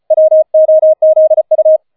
Morse code, heard as a keyed tone, about 34 wpm, 630 Hz. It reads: WOZU